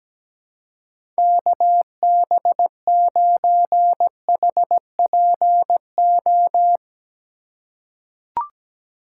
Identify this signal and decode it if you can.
Morse code, heard as a keyed tone, about 17 wpm, 700 Hz. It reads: KB9HPO
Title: USA Call Signs - 6 Characters - 2x Character Spacing - 17wpm